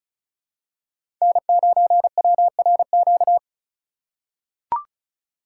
Morse code, heard as a keyed tone, about 34 wpm, 700 Hz. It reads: N9WRQ
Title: USA Call Signs - 5 Characters - 2x Character Spacing - 35wpm